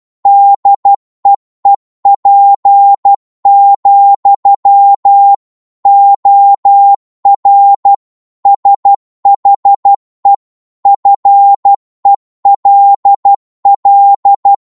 Morse code, heard as a keyed tone, about 12 wpm, 800 Hz. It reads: DEEP, OR SHE FELL
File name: cwmp3 / psg-1-12wpm